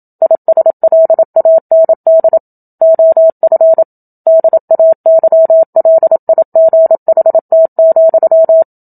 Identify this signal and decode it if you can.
Morse code, heard as a keyed tone, about 27 wpm, 650 Hz. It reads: ISLAND OF DAYLIGHT,
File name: Book Two / 01-Chapter 1 - 27 WPM